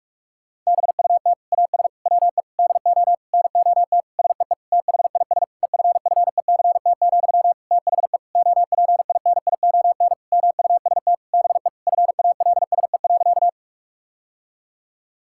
Morse code, heard as a keyed tone, about 46 wpm, 700 Hz. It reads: BUT AS WE DO NOT SEE THIS EFFECT, THE OPINION MUST BE FALSE.
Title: Sentences from Top 1600 Words 45wpm